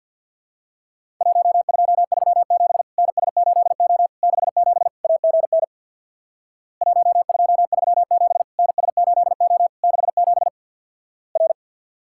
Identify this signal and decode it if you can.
Morse code, heard as a keyed tone, about 50 wpm, 700 Hz. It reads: 1237 NS8O 67 AGN 1237 NS8O 67 R
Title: Worked All Europe DX Contest - QTC Exchange with Repeats - 50wpm